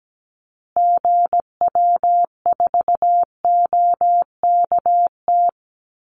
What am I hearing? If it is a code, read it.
Morse code, heard as a keyed tone, about 17 wpm, 700 Hz. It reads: GW4OKT
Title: Call Signs Repeated 2x - 17wpm